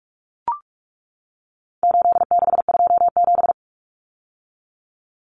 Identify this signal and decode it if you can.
Morse code, heard as a keyed tone, about 44 wpm, 700 Hz. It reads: 8627
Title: Numbers - 4-Digit 45wpm